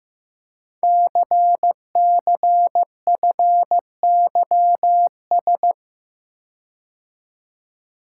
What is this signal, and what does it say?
Morse code, heard as a keyed tone, about 15 wpm, 700 Hz. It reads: CCFYS
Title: Code Groups - 5 Letters - 15wpm